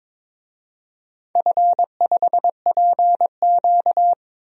Morse code, heard as a keyed tone, about 22 wpm, 700 Hz. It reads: F5PQ
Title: DXCC-100 Call Signs - 4 Characters - 22wpm